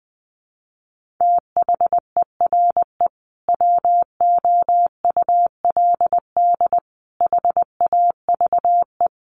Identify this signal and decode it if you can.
Morse code, heard as a keyed tone, about 20 wpm, 700 Hz. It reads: THERE WOULD HAVE
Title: Sentences from Top 1400 Words 20wpm